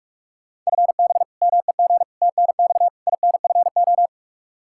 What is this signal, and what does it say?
Morse code, heard as a keyed tone, about 43 wpm, 700 Hz. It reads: FB MEG TNX INFO
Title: Practice QSO - Exchange 3 - More Info - 45wpm